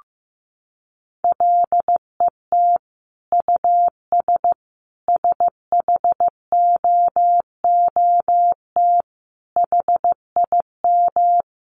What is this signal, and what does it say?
Morse code, heard as a keyed tone, about 15 wpm, 700 Hz. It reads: LET US SHOOT HIM
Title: Sentences from Top 1200 Words 15wpm